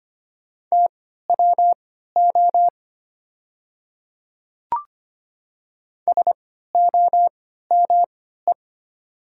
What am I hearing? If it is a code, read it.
Morse code, heard as a keyed tone, about 25 wpm, spaced as for 14 wpm, 700 Hz. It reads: TWO SOME
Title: 3x Character Spacing - Top 100 Words - 25wpm